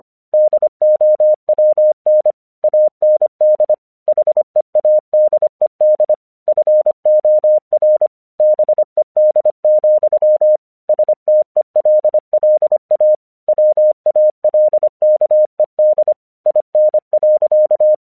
Morse code, heard as a keyed tone, about 25 wpm, 610 Hz. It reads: DOWN AND HEADED FOR BED, STELLA WALKED IN.
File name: 1 English 25wpm / AQC34